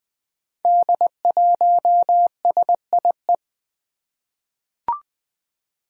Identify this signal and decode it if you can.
Morse code, heard as a keyed tone, about 21 wpm, 700 Hz. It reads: D1SIE
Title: Five Challenging Letter-Number Combinations 20wpm